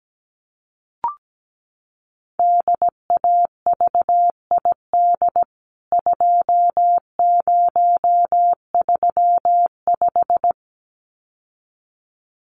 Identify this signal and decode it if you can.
Morse code, heard as a keyed tone, about 17 wpm, 700 Hz. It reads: DAVID 2035
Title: CWT Exchange 17wpm